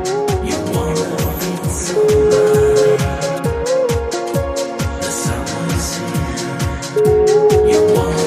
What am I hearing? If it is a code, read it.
Ringback tone (Europe, 425 Hz), with noise.